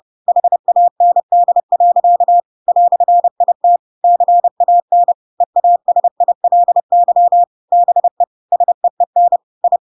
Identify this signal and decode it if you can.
Morse code, heard as a keyed tone, about 30 wpm, 700 Hz. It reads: HAND. "IT CAN EASILY BE SEEN I
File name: Pinocchio / Pinocchio ch1 30 30 ik1hge